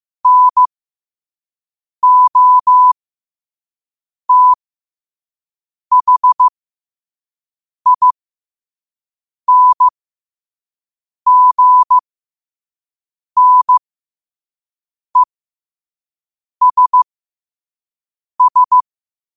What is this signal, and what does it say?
Morse code, heard as a keyed tone, about 15 wpm, spaced as for 5 wpm, 1000 Hz. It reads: NOTHINGNESS
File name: qotdfiles / 5wpm